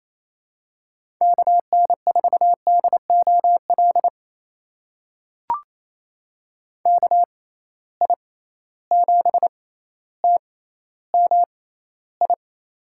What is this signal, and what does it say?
Morse code, heard as a keyed tone, about 27 wpm, spaced as for 14 wpm, 700 Hz. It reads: KN4DOL KI7TMI
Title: USA Call Signs - 6 Characters - 6x Character Spacing - 28wpm